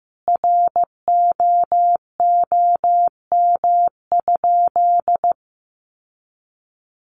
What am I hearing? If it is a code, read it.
Morse code, heard as a keyed tone, about 15 wpm, 700 Hz. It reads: ROOM?
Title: Sentences from Top 1300 Words 15wpm